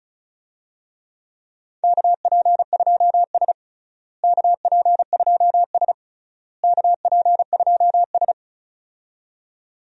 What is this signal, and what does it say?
Morse code, heard as a keyed tone, about 36 wpm, 700 Hz. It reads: KP2S KP2S KP2S
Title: DXCC-100 Call Signs - 4 Characters - Repeated 3x - 35wpm